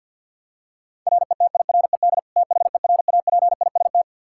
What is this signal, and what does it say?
Morse code, heard as a keyed tone, about 46 wpm, 700 Hz. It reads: RETIRED THERAPIST
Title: Practice QSO - Exchange 3 - More Info - 50wpm